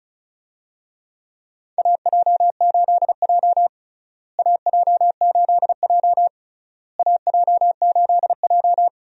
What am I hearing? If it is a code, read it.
Morse code, heard as a keyed tone, about 34 wpm, 700 Hz. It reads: AJ8J AJ8J AJ8J
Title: USA Call Signs - 4 Characters - Repeated 3x - 35wpm